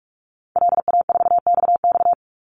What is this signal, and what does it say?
Morse code, heard as a keyed tone, about 44 wpm, 700 Hz. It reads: LA4XX
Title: Call Signs Repeated 3x - 45wpm